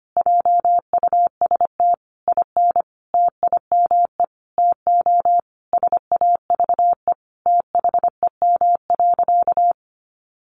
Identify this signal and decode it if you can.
Morse code, heard as a keyed tone, about 25 wpm, 700 Hz. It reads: JUST IN TIME TO SAVE THEM.
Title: Sentences from Top 1100 Words 25wpm